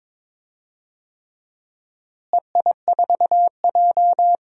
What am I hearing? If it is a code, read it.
Morse code, heard as a keyed tone, about 22 wpm, 700 Hz. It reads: EI4J
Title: DXCC-100 Call Signs - 4 Characters - Repeated 3x - 22wpm